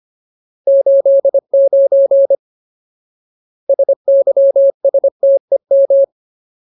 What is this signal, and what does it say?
Morse code, heard as a keyed tone, about 25 wpm, 550 Hz. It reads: 89 SYSTEM
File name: Ultimate Word List / 25WPM Ultimate